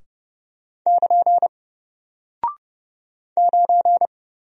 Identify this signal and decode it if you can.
Morse code, heard as a keyed tone, about 30 wpm, 700 Hz. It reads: <KN> 9
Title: Single Character (Letter:Number:Punctuation:Prosign) Alt 30wpm